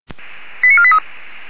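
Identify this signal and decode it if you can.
Five-tone selective call (ZVEI), heard as quick runs of five tones, tones of about 70 ms. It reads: 89483
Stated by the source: Five-tone ZVEI selective call received on an SDR.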